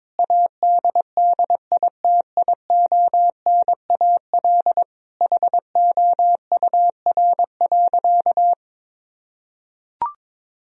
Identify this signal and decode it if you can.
Morse code, heard as a keyed tone, about 22 wpm, 700 Hz. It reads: ADDITIONAL HOUR.
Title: Sentences from Top 1000 Words - Review Part 4 - 22wpm